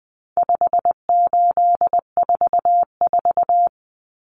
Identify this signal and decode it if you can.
Morse code, heard as a keyed tone, about 19 wpm, 700 Hz. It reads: 5844
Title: Common Cut Numbers - 4-Digit 20wpm